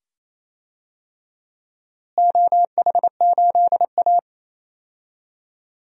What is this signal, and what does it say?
Morse code, heard as a keyed tone, about 28 wpm, 700 Hz. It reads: OH8A
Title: DXCC-100 Call Signs - 4 Characters - Repeated 2x - 28wpm